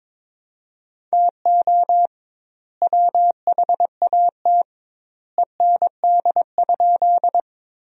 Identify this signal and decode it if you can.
Morse code, heard as a keyed tone, about 22 wpm, 700 Hz. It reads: TO WHAT END?
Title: 2x Word-Spacing - Sentences from Top 200 Words - 22wpm